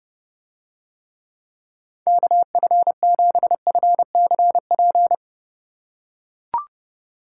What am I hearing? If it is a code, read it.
Morse code, heard as a keyed tone, about 30 wpm, 700 Hz. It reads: KF7FCP